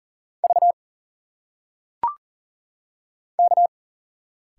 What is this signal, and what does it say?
Morse code, heard as a keyed tone, about 40 wpm, 700 Hz. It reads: V K
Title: Single Letters - Phonetic Pronunciation 40wpm